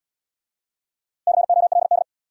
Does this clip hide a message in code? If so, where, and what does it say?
Yes: Morse code from 1.3 to 2.0 s, heard as a keyed tone, about 65 wpm, 700 Hz. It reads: COLD